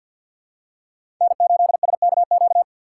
Morse code, heard as a keyed tone, about 53 wpm, 700 Hz. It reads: N8SKQ
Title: USA Call Signs - 5 Characters - 8x Character Spacing - 50wpm